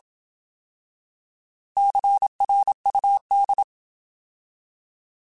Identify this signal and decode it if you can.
Morse code, heard as a keyed tone, about 27 wpm, 800 Hz. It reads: CRUD
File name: Alphabetical / Alphabetical mp3s / Cs @25